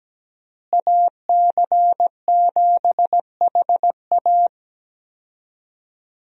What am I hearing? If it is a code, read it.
Morse code, heard as a keyed tone, about 17 wpm, 700 Hz. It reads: AC7HA
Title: USA Call Signs - 5 Characters - Repeated 2x - 17wpm